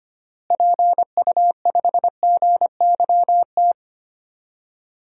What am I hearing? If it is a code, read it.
Morse code, heard as a keyed tone, about 25 wpm, 700 Hz. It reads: PU5GYT
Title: DXCC-100 Call Signs - 6 Characters - Repeated 2x - 25wpm